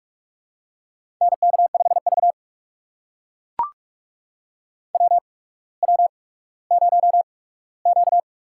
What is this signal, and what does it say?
Morse code, heard as a keyed tone, about 44 wpm, spaced as for 19 wpm, 700 Hz. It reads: NK5V WW0Q
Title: USA Call Signs - 8x Character Spacing - 45wpm